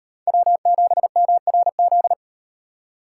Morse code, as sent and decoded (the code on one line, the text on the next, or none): .-- --... -- .-. --..
W7MRZ